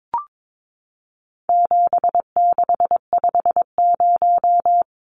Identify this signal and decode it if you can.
Morse code, heard as a keyed tone, about 22 wpm, 700 Hz. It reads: 7650
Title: Numbers - 4-Digit 22wpm